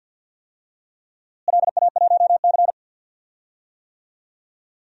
Morse code, heard as a keyed tone, about 50 wpm, 700 Hz. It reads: LA1C
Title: DXCC-100 Call Signs - 4 Characters - Repeated 3x - 50wpm